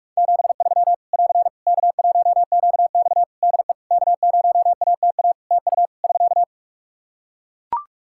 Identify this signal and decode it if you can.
Morse code, heard as a keyed tone, about 41 wpm, 700 Hz. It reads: 73 <AR> K1QX DE K0ATA TU <SK>
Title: Practice QSO - Exchange 4 - Ending - 45wpm